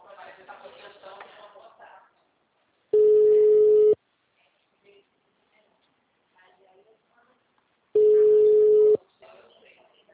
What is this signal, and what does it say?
Ringback tone (Europe, 425 Hz), with speech.